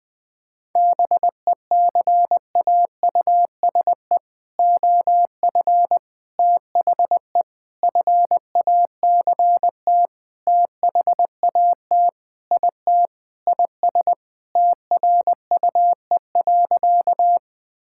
Morse code, heard as a keyed tone, about 21 wpm, 700 Hz. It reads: BECAUSE OF THE FACT THAT IT IS TRUE.
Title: Sentences with 4-Word Phrases - 20wpm